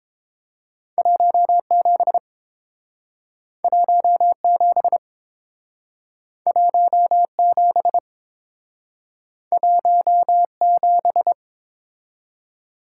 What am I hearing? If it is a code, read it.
Morse code, heard as a keyed tone, about 31 wpm, 700 Hz. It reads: 17 17 17 17